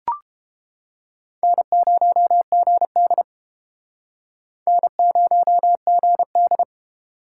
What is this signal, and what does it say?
Morse code, heard as a keyed tone, about 30 wpm, 700 Hz. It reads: N0GD N0GD